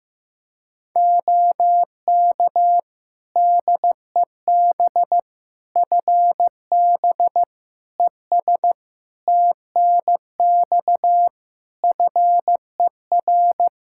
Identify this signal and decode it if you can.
Morse code, heard as a keyed tone, about 16 wpm, 700 Hz. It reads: OK DEB FB ES TNX FER
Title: Practice QSO - Exchange 2 - Info - 15wpm